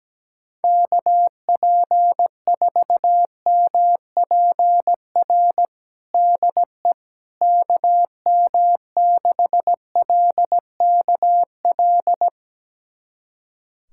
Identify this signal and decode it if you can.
Morse code, heard as a keyed tone, about 17 wpm, 700 Hz. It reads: KP4MPR DE KM6LKL
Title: USA Callsign DE Callsign - 6 Characters - 17wpm